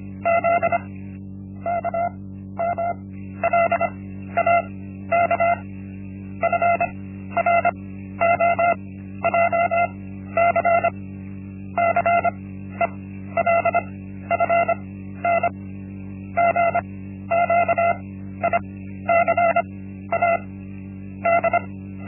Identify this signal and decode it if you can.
Morse code, heard as a keyed tone, about 25 wpm, 680 Hz. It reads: Z K M L A K F R O J C C E L F N G Q I C A D
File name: Linksmakalnis-1